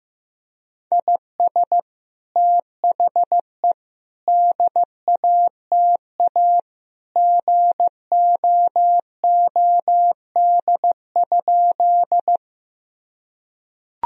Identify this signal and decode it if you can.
Morse code, heard as a keyed tone, about 15 wpm, 700 Hz. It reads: IS THE DATA GOOD?
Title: Sentences from Top 1000 Words - Review Part 3 - 15wpm